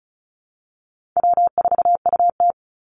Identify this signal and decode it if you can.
Morse code, heard as a keyed tone, about 34 wpm, 700 Hz. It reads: W4UT
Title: Easy Call Signs Repeated 3x - 35wpm